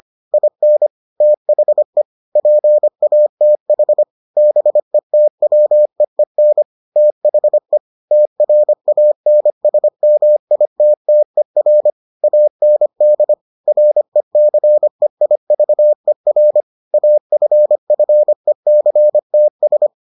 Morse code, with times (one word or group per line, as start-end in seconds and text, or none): 0.3-0.9 IN
1.2-2.0 THE
2.4-4.0 PATH
4.4-6.6 BETWEEN
7.0-7.8 THE
8.1-11.9 TRANSMITTER
12.2-13.3 AND
13.7-16.6 RECEIVER
16.9-19.9 AFFECTS